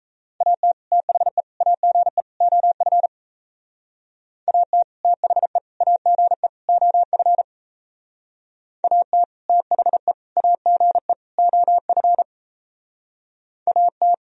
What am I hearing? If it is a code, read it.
Morse code, heard as a keyed tone, about 35 wpm, 700 Hz. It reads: AT THE AGE OF AT THE AGE OF AT THE AGE OF AT
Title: Speed-Racing - Sets of 4 Words 28wpm